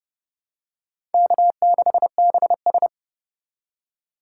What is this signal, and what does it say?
Morse code, heard as a keyed tone, about 32 wpm, 700 Hz. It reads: K6BS
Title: USA Call Signs - 4 Characters - 2x Character Spacing - 30wpm